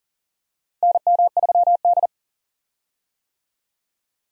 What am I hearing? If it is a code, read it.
Morse code, heard as a keyed tone, about 37 wpm, 700 Hz. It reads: NM3D
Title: USA Call Signs - 4 Characters - 40wpm